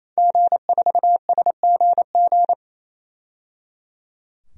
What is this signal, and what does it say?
Morse code, heard as a keyed tone, about 28 wpm, 700 Hz. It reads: G4SGG
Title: DXCC-100 Call Signs - 5 Characters - 28wpm